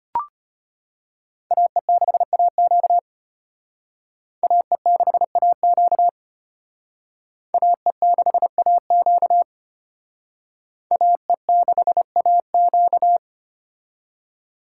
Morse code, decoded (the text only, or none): AE6AQ AE6AQ AE6AQ AE6AQ